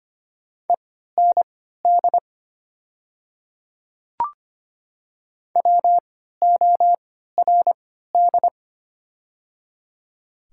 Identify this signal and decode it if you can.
Morse code, heard as a keyed tone, about 25 wpm, spaced as for 14 wpm, 690 Hz. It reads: END WORD
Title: 3x Character Spacing - Top 200 Words - 25wpm